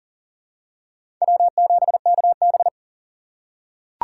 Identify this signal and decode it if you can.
Morse code, heard as a keyed tone, about 40 wpm, 700 Hz. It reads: W7KB
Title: USA Call Signs - 4x Character Spacing - 40wpm